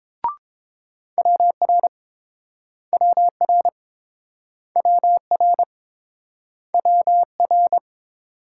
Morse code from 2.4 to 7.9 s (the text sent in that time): WR WR WR